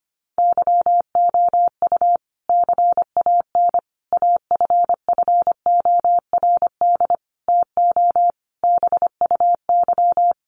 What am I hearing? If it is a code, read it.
Morse code, heard as a keyed tone, about 25 wpm, 700 Hz. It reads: YOU CAN AFFORD TO BUY
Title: Sentences from Top 1100 Words 25wpm